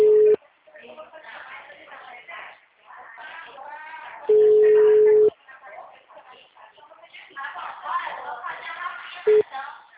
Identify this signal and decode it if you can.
Ringback tone (Europe, 425 Hz), with speech.
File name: out- -2017